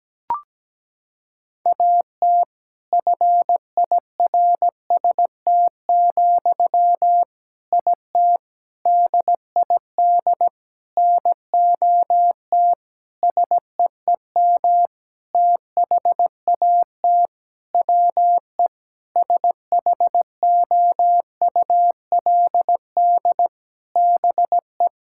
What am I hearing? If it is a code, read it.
Morse code, heard as a keyed tone, about 17 wpm, 700 Hz. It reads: AT FIRST, IT DID NOT SEEM THAT WE SHOULD BE
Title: Sentences from Top 1000 Words - Review Part 6 - 17wpm